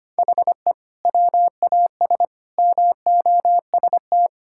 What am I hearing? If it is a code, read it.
Morse code, heard as a keyed tone, about 25 wpm, 700 Hz. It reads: HE WAS MOST
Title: Sentences from Top 1000 Words - Review Part 6 - 25wpm